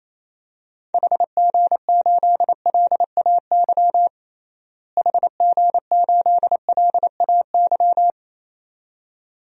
Morse code, heard as a keyed tone, about 28 wpm, 700 Hz. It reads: HG8LAY HG8LAY
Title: DXCC-100 Call Signs - 6 Characters - Repeated 2x - 28wpm